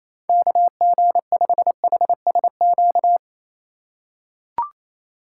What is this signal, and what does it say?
Morse code, heard as a keyed tone, about 28 wpm, 700 Hz. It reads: KG5HSQ
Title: USA Call Signs - 4x Character Spacing - 28wpm